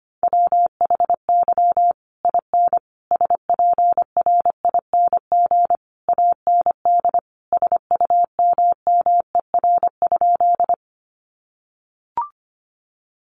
Morse code, heard as a keyed tone, about 25 wpm, 700 Hz. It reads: WHY IN SPRING AND SUMMER?
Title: Sentences from Top 1200 Words 25wpm